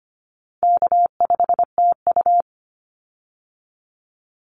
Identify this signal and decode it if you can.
Morse code, heard as a keyed tone, about 25 wpm, 700 Hz. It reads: K5TU
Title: Call Signs Repeated 3x - 25wpm